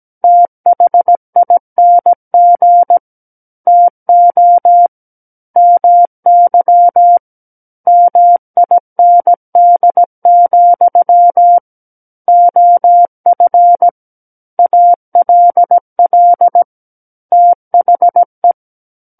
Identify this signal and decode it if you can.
Morse code, heard as a keyed tone, about 17 wpm, 700 Hz. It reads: THING TO MY MIND, OF ALL THE
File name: Book One / 08-Chapter 8 - 17 WPM